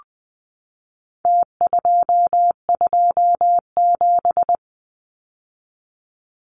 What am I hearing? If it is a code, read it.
Morse code, heard as a keyed tone, about 20 wpm, 700 Hz. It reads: T227
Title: Numbers - Contest Style - 4-Digit 20wpm